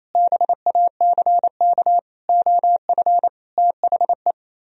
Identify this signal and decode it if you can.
Morse code, heard as a keyed tone, about 34 wpm, 700 Hz. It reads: BACK OF THE
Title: Speed-Racing - Sets of 5 Words 28wpm